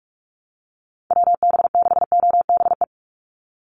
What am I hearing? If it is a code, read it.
Morse code, heard as a keyed tone, about 44 wpm, 700 Hz. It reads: WB6OBE
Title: USA Call Signs 45wpm